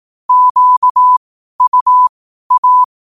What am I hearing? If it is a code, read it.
Morse code, heard as a keyed tone, about 18 wpm, spaced as for 12 wpm, 1000 Hz. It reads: QUA
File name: qotdfiles / 12wpm